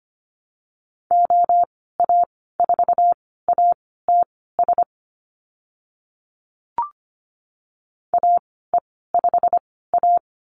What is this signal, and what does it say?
Morse code, heard as a keyed tone, about 25 wpm, spaced as for 16 wpm, 700 Hz. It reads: OA4ATS AE5A
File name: Lesson 15g - Number 5 - Call Signs 25wpm